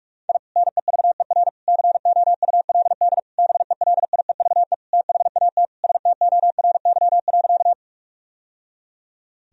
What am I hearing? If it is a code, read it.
Morse code, heard as a keyed tone, about 44 wpm, 700 Hz. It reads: I NEVER COULD BELIEVE THAT STORY.
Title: Sentences from Top 1000 Words - Review Part 1 - 45wpm